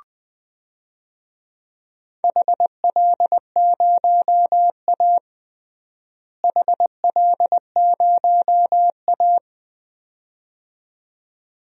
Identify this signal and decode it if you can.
Morse code, heard as a keyed tone, about 20 wpm, 700 Hz. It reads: HL0A HL0A
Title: DXCC-100 Call Signs - 4 Characters - Repeated 2x - 20wpm